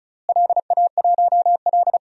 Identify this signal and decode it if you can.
Morse code, heard as a keyed tone, about 36 wpm, 700 Hz. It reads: LA1L
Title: DXCC-100 Call Signs - 4 Characters - Repeated 3x - 35wpm